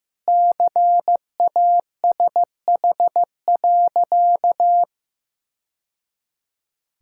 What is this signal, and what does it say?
Morse code, heard as a keyed tone, about 15 wpm, 700 Hz. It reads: CASH.
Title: Sentences from Top 1000 Words - Review Part 5 - 15wpm